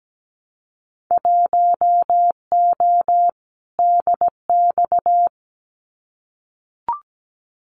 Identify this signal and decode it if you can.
Morse code, heard as a keyed tone, about 17 wpm, 700 Hz. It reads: 1O DX
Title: Winter Field Day Exchanges Encore 17wpm